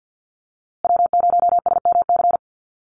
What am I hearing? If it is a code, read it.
Morse code, heard as a keyed tone, about 50 wpm, 700 Hz. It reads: W0SMC
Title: USA Call Signs Repeated 2x 50wpm